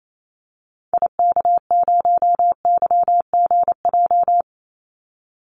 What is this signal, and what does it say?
Morse code, heard as a keyed tone, about 28 wpm, 700 Hz. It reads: IK0YGJ
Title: Call Signs Repeated 2x - 28wpm